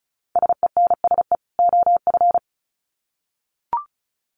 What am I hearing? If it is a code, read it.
Morse code, heard as a keyed tone, about 34 wpm, 700 Hz. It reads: SENSE OF
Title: Sets of 2 Words 35wpm